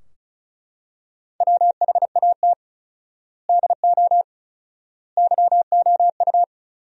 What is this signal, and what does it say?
Morse code, heard as a keyed tone, about 34 wpm, 700 Hz. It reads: WHAT DO YOU
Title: 4x Word-Spacing - Sentences from Top 200 Words - 35wpm